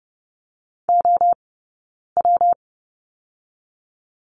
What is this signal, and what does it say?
Morse code, heard as a keyed tone, about 30 wpm, spaced as for 9 wpm, 700 Hz. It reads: OW
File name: Lesson 20d - Number 3 - Two Characters 30wpm